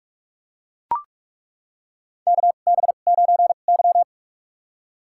Morse code, heard as a keyed tone, about 43 wpm, spaced as for 31 wpm, 700 Hz. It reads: KB9Y
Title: USA Call Signs - 4 Characters - 2x Character Spacing - 45wpm